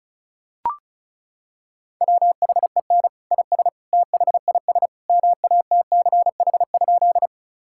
Morse code, heard as a keyed tone, about 36 wpm, 700 Hz. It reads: WHEN IS THIS MATCH?
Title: Sentences from Top 1600 Words 35wpm